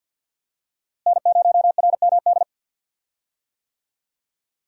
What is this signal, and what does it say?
Morse code, heard as a keyed tone, about 52 wpm, 700 Hz. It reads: N0RMD
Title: USA Call Signs - 5 Characters - Repeated 3x - 50wpm